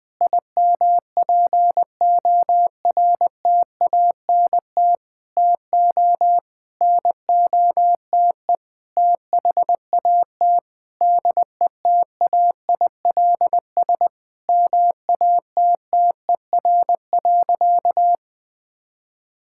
Morse code, heard as a keyed tone, about 20 wpm, 700 Hz. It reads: IMPORTANT TO NOTE THAT DETAILS MATTER.
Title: Sentences with 5-Word Phrases - Spoken Beginning - 20wpm